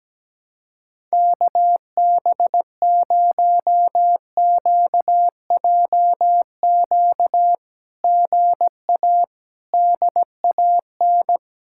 Morse code, heard as a keyed tone, about 17 wpm, 700 Hz. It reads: KB0QJQ GA DAN
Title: SOTA QSO Practice 17wpm